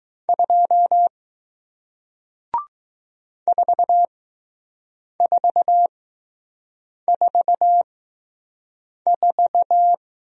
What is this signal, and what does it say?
Morse code, heard as a keyed tone, about 20 wpm, spaced as for 6 wpm, 700 Hz. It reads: 2 4444